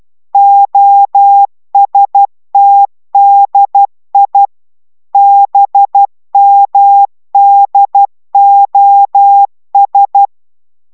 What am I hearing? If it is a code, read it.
Morse code, heard as a keyed tone, about 12 wpm, 800 Hz. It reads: OSTDI BMDOS